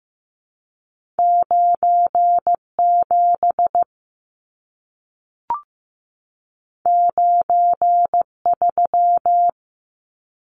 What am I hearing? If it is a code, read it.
Morse code, heard as a keyed tone, about 15 wpm, 700 Hz. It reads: 9 7 9 3
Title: Numbers - 2-Digit 15wpm